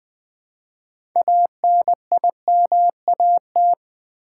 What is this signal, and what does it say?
Morse code, heard as a keyed tone, about 20 wpm, 700 Hz. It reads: ANIMAT
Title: Build A Word - 3 Letters At A Time - 9-Letter Words - 20wpm